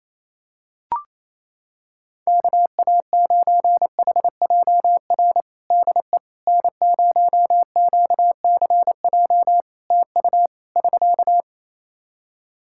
Morse code, heard as a keyed tone, about 28 wpm, 700 Hz. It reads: KA9HJR DE N0QCJ TU <SK>
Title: Practice QSO - Exchange 4 - Ending - 28wpm